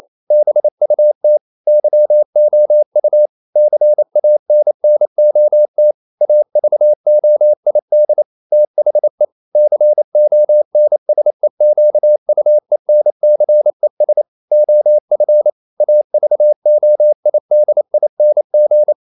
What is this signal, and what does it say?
Morse code, heard as a keyed tone, about 28 wpm, 600 Hz. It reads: BUT YOU CANNOT AVOID THE CONSEQUENCES OF AVOIDING
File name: CW / Practice / EnglishQuotations0128